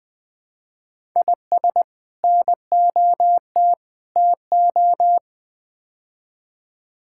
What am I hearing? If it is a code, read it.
Morse code, heard as a keyed tone, about 21 wpm, 700 Hz. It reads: IS NOT TO
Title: Build A Phrase - 5-Word Phrases - 20wpm